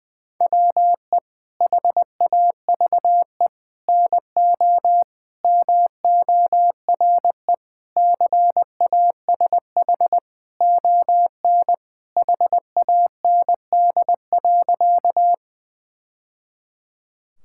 Morse code, heard as a keyed tone, about 21 wpm, 700 Hz. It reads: WE HAVE NO MORE CASH ON HAND.
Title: Sentences from Top 1000 Words - Review Part 5 - 20wpm